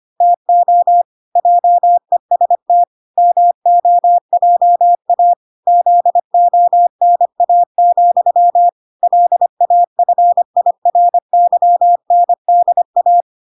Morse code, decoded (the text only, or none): TO JEST MOJA ZONA, LAFIRYNDA